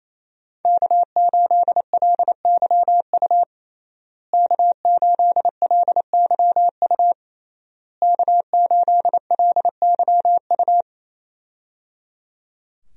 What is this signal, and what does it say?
Morse code, heard as a keyed tone, about 28 wpm, 700 Hz. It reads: K8LYU K8LYU K8LYU
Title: USA Call Signs - 5 Characters - Repeated 3x - 28wpm